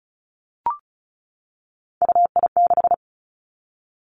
Morse code, heard as a keyed tone, about 36 wpm, 700 Hz. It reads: UI6
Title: Three Challenging Letter-Number Combinations 35wpm